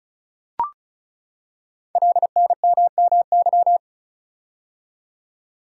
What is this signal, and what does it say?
Morse code, heard as a keyed tone, about 36 wpm, 690 Hz. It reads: LNMMY
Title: Code Groups - 5 Letters - 35wpm